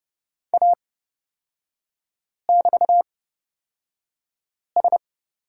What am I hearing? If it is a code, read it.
Morse code, heard as a keyed tone, about 30 wpm, spaced as for 5 wpm, 700 Hz. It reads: A<BT>S
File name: Lesson 38e - Letter q - Three Characters 30wpm